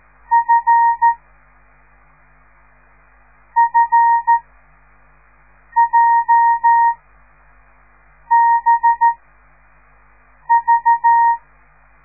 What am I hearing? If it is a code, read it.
Morse code, heard as a keyed tone, about 13 wpm, 930 Hz. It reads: F F J B V